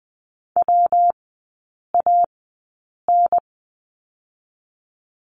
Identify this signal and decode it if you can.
Morse code, heard as a keyed tone, about 20 wpm, spaced as for 8 wpm, 700 Hz. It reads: WAN
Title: 159 Lesson 26e - Letter g - Three Characters 20wpm